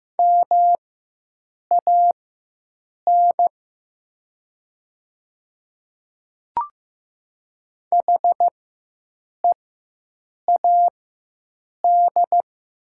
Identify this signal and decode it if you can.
Morse code, heard as a keyed tone, about 15 wpm, spaced as for 7 wpm, 700 Hz. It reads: MAN HEAD
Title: 4x Character Spacing - Top 300 Words - 15wpm